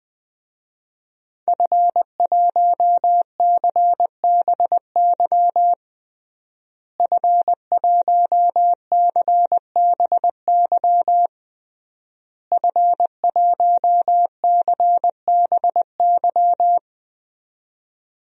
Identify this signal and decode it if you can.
Morse code, heard as a keyed tone, about 21 wpm, 700 Hz. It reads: F1CBY F1CBY F1CBY
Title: DXCC-100 Call Signs - 5 Characters - Repeated 3x - 20wpm